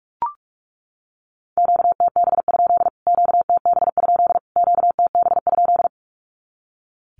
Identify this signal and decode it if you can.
Morse code, heard as a keyed tone, about 44 wpm, 700 Hz. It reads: QTB? QTB? QTB?
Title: Q-Codes Repeated 3x 45wpm